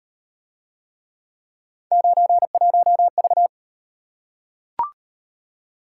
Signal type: Morse code

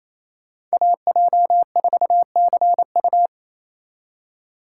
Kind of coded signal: Morse code

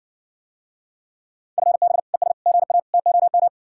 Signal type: Morse code

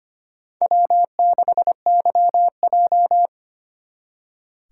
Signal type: Morse code

